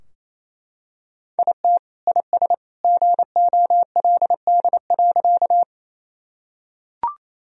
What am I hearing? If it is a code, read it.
Morse code, heard as a keyed tone, about 28 wpm, 700 Hz. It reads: IT IS GOLD.
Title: Sentences from Top 1000 Words - Review Part 3 - 28wpm